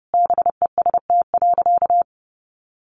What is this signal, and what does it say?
Morse code, heard as a keyed tone, about 30 wpm, 700 Hz. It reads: BEST.
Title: Sentences from Top 600 Words 30wpm